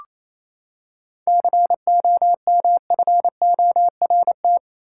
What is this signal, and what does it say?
Morse code, heard as a keyed tone, about 28 wpm, 700 Hz. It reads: COMFORT
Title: Seven Letter Words - 28wpm